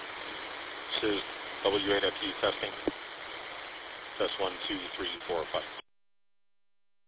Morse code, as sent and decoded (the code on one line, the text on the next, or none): none